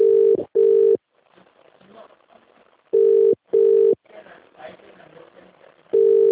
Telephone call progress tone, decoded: ringback tone